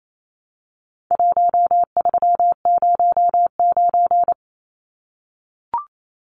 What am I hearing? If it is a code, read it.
Morse code, heard as a keyed tone, about 28 wpm, 700 Hz. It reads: 1309
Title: Numbers - 4-Digit 28wpm